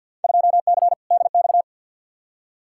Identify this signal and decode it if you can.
Morse code, heard as a keyed tone, about 52 wpm, 700 Hz. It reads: 2C DX
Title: Field Day Exchanges Encore - 50wpm